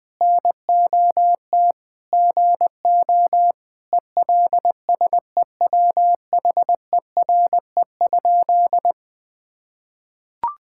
Morse code, heard as a keyed tone, about 20 wpm, 700 Hz. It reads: NOT GO ELSEWHERE?
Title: Sentences from Top 1600 Words 20wpm